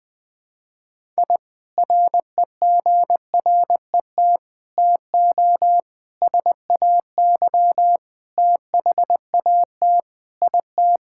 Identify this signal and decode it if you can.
Morse code, heard as a keyed tone, about 19 wpm, 700 Hz. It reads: I REGRET TO SAY THAT IT
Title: Sentences with 3-Word Phrases - Spoken Beginning - 20wpm